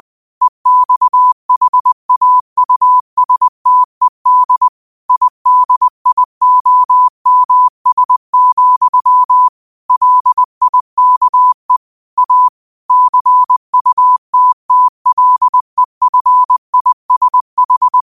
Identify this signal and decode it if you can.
Morse code, heard as a keyed tone, about 20 wpm, 1000 Hz. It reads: EXHAUSTED IDIOMS, LIKE A CUTTLEFISH